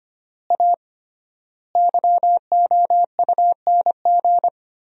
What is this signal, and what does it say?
Morse code, heard as a keyed tone, about 25 wpm, 700 Hz. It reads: A YOUNG
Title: 3x Word-Spacing - Sentences from Top 500 Words - 25wpm